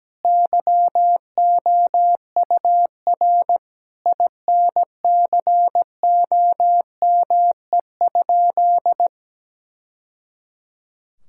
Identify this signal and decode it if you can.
Morse code, heard as a keyed tone, about 17 wpm, 700 Hz. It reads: YOUR INCOME?
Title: Sentences from Top 1000 Words - Review Part 5 - 17wpm